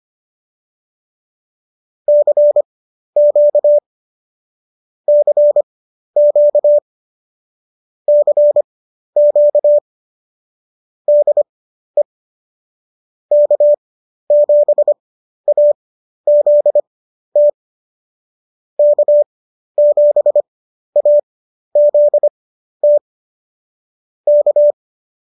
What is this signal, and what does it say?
Morse code, heard as a keyed tone, about 25 wpm, spaced as for 12 wpm, 590 Hz. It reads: CQ CQ CQ DE K7AZT K7AZT K